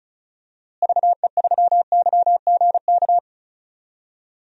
Morse code, heard as a keyed tone, about 34 wpm, 700 Hz. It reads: VE3YGK
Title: DXCC-100 Call Signs - 6 Characters - Repeated 3x - 35wpm